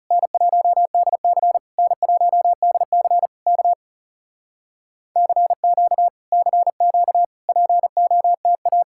Morse code, heard as a keyed tone, about 40 wpm, 700 Hz. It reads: N1DC N1DC K CQ CQ POTA